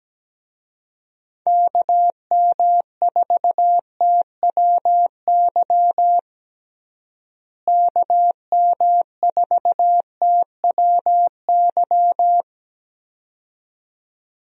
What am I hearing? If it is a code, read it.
Morse code, heard as a keyed tone, about 17 wpm, 700 Hz. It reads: KM4TWY KM4TWY
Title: USA Call Signs - 6 Characters - Repeated 2x - 17wpm